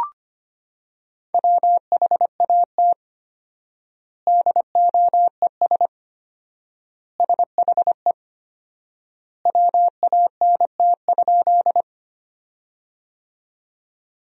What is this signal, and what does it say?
Morse code, heard as a keyed tone, about 25 wpm, 700 Hz. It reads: WHAT DOES SHE WANT?
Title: 4x Word-Spacing - Sentences from Top 200 Words - 25wpm